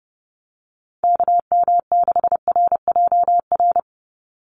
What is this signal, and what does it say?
Morse code, heard as a keyed tone, about 29 wpm, 700 Hz. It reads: KM6RJR
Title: USA Call Signs 30wpm